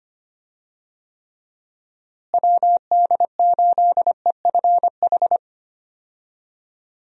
Morse code, heard as a keyed tone, about 25 wpm, 700 Hz. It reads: WD8EFH
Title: USA Call Signs - 6 Characters - Repeated 3x - 25wpm